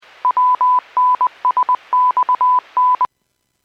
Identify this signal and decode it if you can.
Morse code, heard as a keyed tone, about 20 wpm, 1000 Hz. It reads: WNSXN